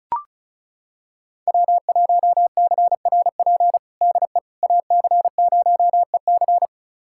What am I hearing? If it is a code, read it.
Morse code, heard as a keyed tone, about 36 wpm, 690 Hz. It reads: W1CRP DE AC0EC
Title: USA Callsign DE Callsign - 5 Characters - 35wpm